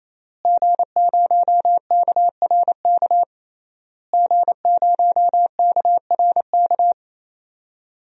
Morse code, heard as a keyed tone, about 28 wpm, 700 Hz. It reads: G0KRK G0KRK
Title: DXCC-100 Call Signs - 5 Characters - Repeated 3x - 28wpm